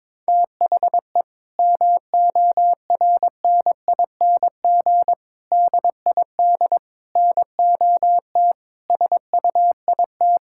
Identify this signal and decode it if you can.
Morse code, heard as a keyed tone, about 22 wpm, 700 Hz. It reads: THE MORNING DID NOT SUIT
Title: Sentences from Top 1600 Words 22wpm